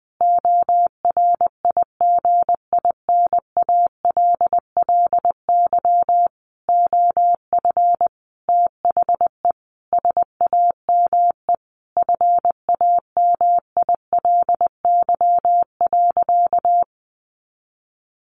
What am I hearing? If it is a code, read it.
Morse code, heard as a keyed tone, about 20 wpm, 700 Hz. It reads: ORIGINALLY OF THE SAME FAMILY.
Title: Sentences from Top 1500 Words 20wpm